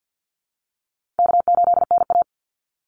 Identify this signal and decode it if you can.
Morse code, heard as a keyed tone, about 49 wpm, 700 Hz. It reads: K8NA
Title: Easy Call Signs Repeated 2x - 50wpm